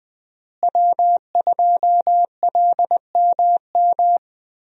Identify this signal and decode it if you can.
Morse code, heard as a keyed tone, about 21 wpm, 700 Hz. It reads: W2LMM
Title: USA Call Signs - 5 Characters - Repeated 2x - 20wpm